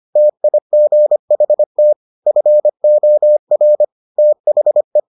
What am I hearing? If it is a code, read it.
Morse code, heard as a keyed tone, about 25 wpm, 600 Hz. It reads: TIGHT FOR THE